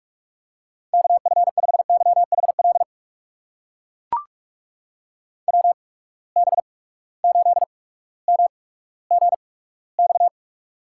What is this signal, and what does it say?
Morse code, heard as a keyed tone, about 43 wpm, spaced as for 19 wpm, 700 Hz. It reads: KF5YHL WB8MGX
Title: USA Call Signs - 6 Characters - 8x Character Spacing - 45wpm